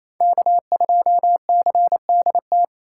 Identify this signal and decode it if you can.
Morse code, heard as a keyed tone, about 35 wpm, 700 Hz. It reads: K2CDT
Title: Speed Racing - USA 1x3 Call Signs 28wpm